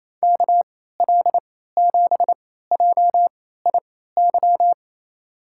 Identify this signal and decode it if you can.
Morse code, heard as a keyed tone, about 27 wpm, spaced as for 16 wpm, 700 Hz. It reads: KL7JIY
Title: USA Call Signs - 3x Character Spacing - 28wpm